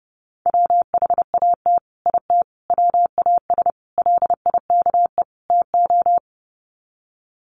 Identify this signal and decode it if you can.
Morse code, heard as a keyed tone, about 30 wpm, 700 Hz. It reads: WHAT IT WAS LIKE TO
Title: Sets of 5 Words Encore 30wpm